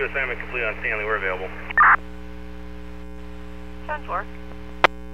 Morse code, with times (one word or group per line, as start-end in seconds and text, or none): none